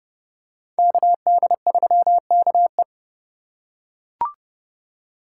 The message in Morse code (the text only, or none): KD3KE